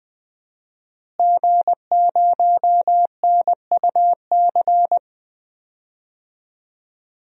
Morse code, heard as a keyed tone, about 20 wpm, 700 Hz. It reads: G0NUC
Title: DXCC-100 Call Signs - 5 Characters - Repeated 2x - 20wpm